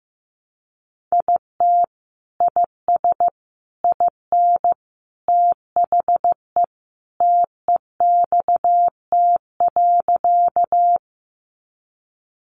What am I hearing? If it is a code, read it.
Morse code, heard as a keyed tone, about 15 wpm, 700 Hz. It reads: IT IS IN THE TEXT.
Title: Sentences from Top 1200 Words 15wpm